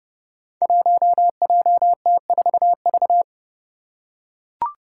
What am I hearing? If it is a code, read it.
Morse code, heard as a keyed tone, about 30 wpm, 700 Hz. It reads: 1JT4V
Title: Five Challenging Letter-Number Combinations 30wpm